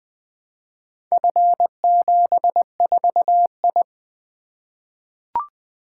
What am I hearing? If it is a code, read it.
Morse code, heard as a keyed tone, about 20 wpm, 700 Hz. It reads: F74I
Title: Four Letter-Number Combinations - 20wpm